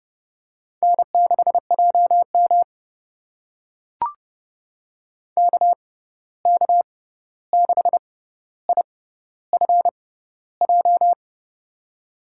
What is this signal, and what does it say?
Morse code, heard as a keyed tone, about 32 wpm, spaced as for 14 wpm, 700 Hz. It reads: N6JM KK6IFJ
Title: USA Call Signs - 6x Character Spacing - 30wpm